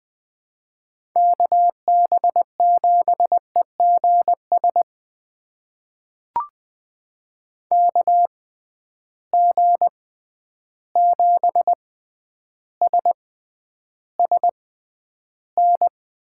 Morse code, heard as a keyed tone, about 20 wpm, spaced as for 10 wpm, 700 Hz. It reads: KB7EGS KG7SSN
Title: USA Call Signs - 6x Character Spacing - 20wpm